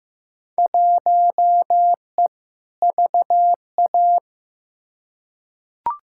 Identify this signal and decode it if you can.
Morse code, heard as a keyed tone, about 15 wpm, 700 Hz. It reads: 1E VA
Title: Field Day Exchanges - 15wpm